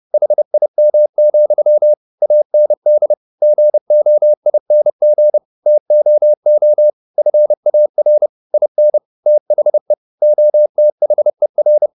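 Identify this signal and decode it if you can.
Morse code, heard as a keyed tone, about 30 wpm, 600 Hz. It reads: HIM, AND GOING TOO FAR IN THE OTHER
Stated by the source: Morse code at 30 wpm, the novel Never by Ken Follett in English, from CW med Gnister.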